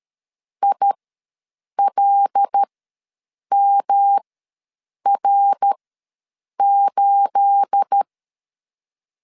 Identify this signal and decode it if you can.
Morse code, heard as a keyed tone, about 13 wpm, spaced as for 7 wpm, 800 Hz. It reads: ILMR8
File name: LEZIONI / 9 28 ws 12wpm 3354